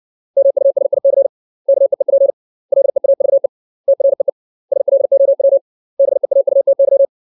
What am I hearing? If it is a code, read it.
Morse code, heard as a keyed tone, about 63 wpm, 550 Hz. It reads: MUSIC KEEP LEAVE TREE SLOW BEAUTY
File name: 632 words / 632 Words 60WPM